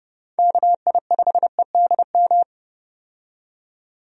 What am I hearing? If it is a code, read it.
Morse code, heard as a keyed tone, about 30 wpm, 700 Hz. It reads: KI5EDM
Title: USA Call Signs - 6 Characters - 30wpm